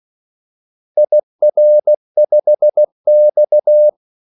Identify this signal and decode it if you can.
Morse code, heard as a keyed tone, about 16 wpm, 600 Hz. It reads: IR5X
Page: Kaldesignaler 16 wpm